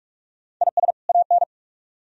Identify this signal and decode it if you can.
Morse code, heard as a keyed tone, about 44 wpm, 700 Hz. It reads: IS AN